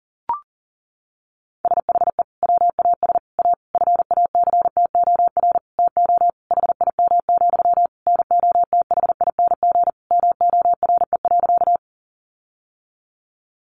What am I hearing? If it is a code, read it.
Morse code, heard as a keyed tone, about 37 wpm, 700 Hz. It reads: SHE WAS A FACTOR TO HIM, NOTHING MORE.
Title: Sentences from Top 1300 Words 40wpm